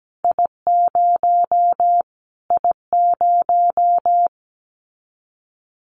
Morse code, heard as a keyed tone, about 17 wpm, 700 Hz. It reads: I0 I0
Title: Callsign Prefixes for Top 25 DXCC - Repeated 3x - 17wpm